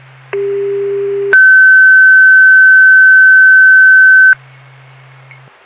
Two-tone page, heard as a one-second tone, then a three-second tone, about 400 then 1550 Hz. It is heard from 0.3 to 4.3 s.